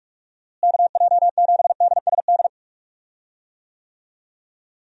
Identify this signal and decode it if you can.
Morse code, heard as a keyed tone, about 41 wpm, 700 Hz. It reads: KJ7DSD
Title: USA Call Signs - 6 Characters - Repeated 3x - 45wpm